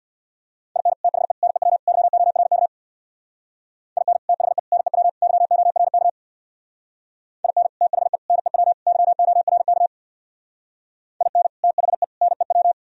Morse code, heard as a keyed tone, about 64 wpm, 700 Hz. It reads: IN THE NEW YORK IN THE NEW YORK IN THE NEW YORK IN THE NEW